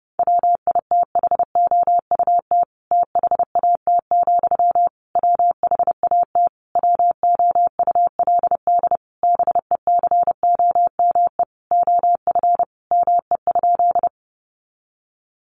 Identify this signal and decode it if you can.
Morse code, heard as a keyed tone, about 30 wpm, 700 Hz. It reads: WITHOUT THAT, WHAT WOULD BECOME OF ME?
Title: Sentences from Top 600 Words 30wpm